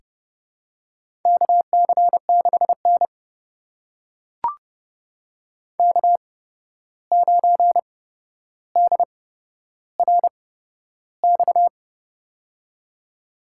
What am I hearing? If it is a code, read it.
Morse code, heard as a keyed tone, about 30 wpm, spaced as for 12 wpm, 700 Hz. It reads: KC6N K9DRX
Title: USA Call Signs - 8x Character Spacing - 30wpm